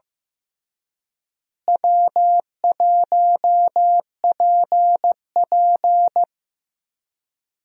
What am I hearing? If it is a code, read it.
Morse code, heard as a keyed tone, about 15 wpm, 700 Hz. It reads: W1PP
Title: USA Call Signs - 4 Characters - Repeated 2x - 15wpm